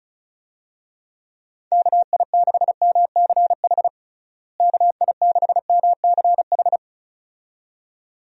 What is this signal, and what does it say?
Morse code, heard as a keyed tone, about 34 wpm, 700 Hz. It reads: KI6MCH KI6MCH
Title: USA Call Signs - 6 Characters - Repeated 2x - 35wpm